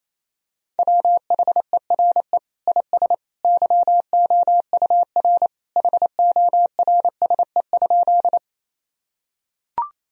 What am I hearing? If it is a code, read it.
Morse code, heard as a keyed tone, about 28 wpm, 700 Hz. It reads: WHERE IS YOUR HORSE?
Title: Sentences from Top 1000 Words - Review Part 1 - 28wpm